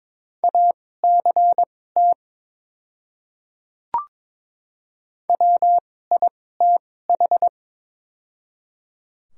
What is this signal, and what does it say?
Morse code, heard as a keyed tone, about 22 wpm, spaced as for 16 wpm, 700 Hz. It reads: ACT WITH
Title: 2x Character Spacing - Top 200 Words - 22wpm